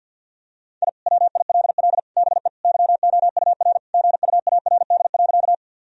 Morse code, heard as a keyed tone, about 50 wpm, 700 Hz. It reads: I WILL BE YOUR GUARD.
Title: Sentences from Top 1000 Words - Review Part 4 - 50wpm